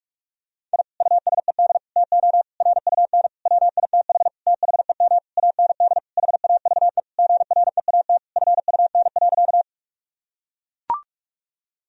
Morse code, heard as a keyed tone, about 43 wpm, 700 Hz. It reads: I USED TO RUN WITH THEM AND HAVE GREAT FUN.
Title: Sentences from Top 1000 Words - Review Part 4 - 45wpm